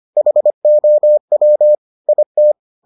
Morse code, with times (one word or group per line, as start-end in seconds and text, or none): 0.2-1.8 HOW
2.1-2.5 IT